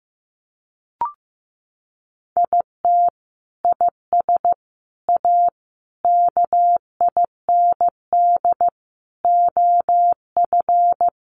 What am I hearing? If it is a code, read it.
Morse code, heard as a keyed tone, about 15 wpm, 700 Hz. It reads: IT IS A KIND OF
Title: Sentences from Top 1500 Words 15wpm